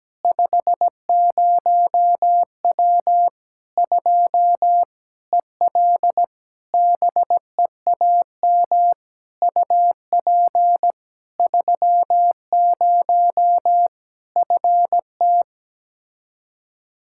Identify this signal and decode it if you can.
Morse code, heard as a keyed tone, about 17 wpm, 700 Hz. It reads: 50W 2 EL BEAM UP 30 FT
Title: Practice QSO - Exchange 2 - Info - 17wpm